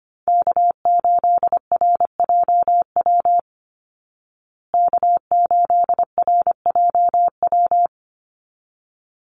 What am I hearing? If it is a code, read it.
Morse code, heard as a keyed tone, about 25 wpm, 700 Hz. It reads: K8RJW K8RJW
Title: Call Signs Repeated 3x - 25wpm